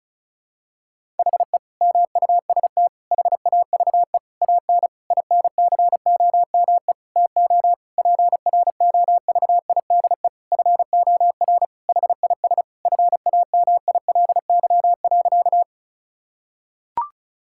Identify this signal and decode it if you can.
Morse code, heard as a keyed tone, about 36 wpm, 700 Hz. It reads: HE MUST HAVE AN INCOME TO PROVIDE FOR HIS FAMILY.
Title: Sentences from Top 1000 Words - Review Part 5 - 35wpm